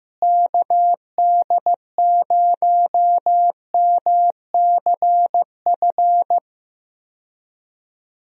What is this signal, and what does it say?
Morse code, heard as a keyed tone, about 15 wpm, 700 Hz. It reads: KD0MCF
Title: USA Call Signs - 6 Characters - Repeated 2x - 15wpm